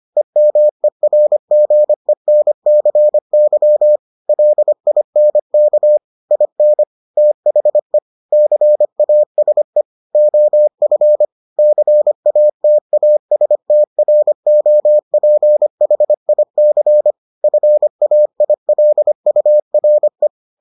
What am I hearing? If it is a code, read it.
Morse code, heard as a keyed tone, about 25 wpm, 600 Hz. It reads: EMERGENCY LINK IN THE CASE OF CATASTROPHIC FAILURE